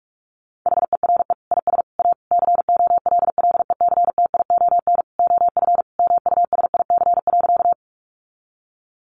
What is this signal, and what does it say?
Morse code, heard as a keyed tone, about 44 wpm, 700 Hz. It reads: HERE IS A COLLECTION OF MUSIC.
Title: Sentences from Top 1200 Words 45wpm